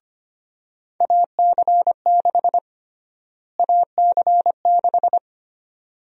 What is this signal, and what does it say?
Morse code, heard as a keyed tone, about 25 wpm, 700 Hz. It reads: AC6 AC6
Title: USA Call Sign Prefixes - Repeated 2x - 25wpm